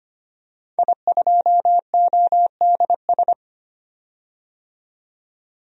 Morse code, heard as a keyed tone, about 25 wpm, 700 Hz. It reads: I2ODS
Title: DXCC-100 Call Signs - 5 Characters - 25wpm